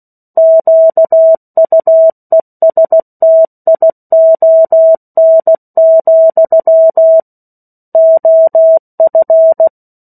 Morse code, heard as a keyed tone, about 16 wpm, 650 Hz. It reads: QUESTION, OF
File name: 03-Chapter-03 16 WPM